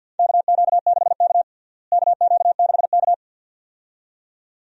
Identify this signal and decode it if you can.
Morse code, heard as a keyed tone, about 53 wpm, 690 Hz. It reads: KQ6K KQ6K
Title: USA Call Signs - 4 Characters - Repeated 2x - 50wpm